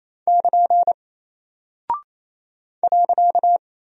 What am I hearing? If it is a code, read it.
Morse code, heard as a keyed tone, about 28 wpm, 700 Hz. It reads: <KN> .
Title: Single Character (Letter:Number:Punctuation:Prosign) Alt 28wpm